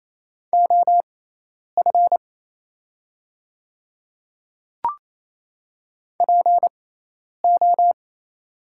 Morse code, heard as a keyed tone, about 28 wpm, spaced as for 10 wpm, 700 Hz. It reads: OF PO
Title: 6x Character Spacing - Top 200 Words - 28wpm